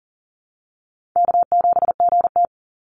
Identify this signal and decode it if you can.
Morse code, heard as a keyed tone, about 40 wpm, 700 Hz. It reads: K7GT
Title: Easy Call Signs Repeated 2x - 40wpm